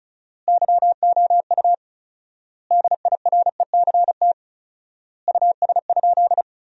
Morse code, heard as a keyed tone, about 36 wpm, 700 Hz. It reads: YOU DIRECT US?
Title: 4x Word-Spacing - Sentences from Top 400 Words - 35wpm